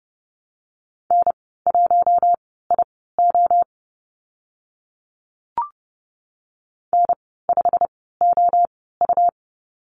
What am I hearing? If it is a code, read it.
Morse code, heard as a keyed tone, about 32 wpm, spaced as for 17 wpm, 700 Hz. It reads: N1IO N5OU
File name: Lesson 19g - Letter w - Call Signs 30wpm